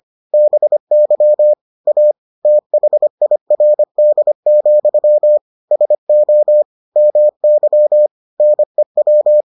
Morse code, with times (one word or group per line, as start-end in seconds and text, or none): 0.3-1.5 BY
1.9-2.1 A
2.5-5.4 THIRD,
5.7-6.6 SO
7.0-8.1 MY
8.4-9.4 NEW